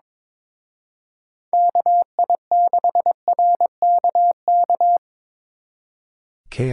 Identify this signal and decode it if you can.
Morse code, heard as a keyed tone, about 26 wpm, 700 Hz. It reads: KI6RKK
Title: Speed Racing - USA 2x3 Call Signs 22wpm